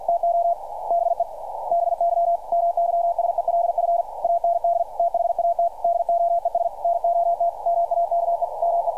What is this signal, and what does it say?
Morse code, heard as a keyed tone, about 48 wpm, 690 Hz. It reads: TO BE BY MYSELF AND THAT DOES NOT MAKE ME
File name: archived / CRSnet